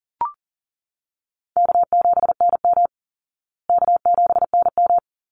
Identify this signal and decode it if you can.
Morse code, heard as a keyed tone, about 40 wpm, 700 Hz. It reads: K7NM K7NM
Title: Easy Call Signs Repeated 2x - 40wpm